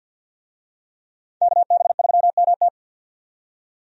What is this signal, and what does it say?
Morse code, heard as a keyed tone, about 51 wpm, 700 Hz. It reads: KB3MT
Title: USA Call Signs - 5 Characters - 50wpm